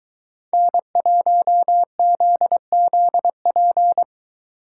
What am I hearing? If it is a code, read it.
Morse code, heard as a keyed tone, about 23 wpm, 700 Hz. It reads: N1ZZP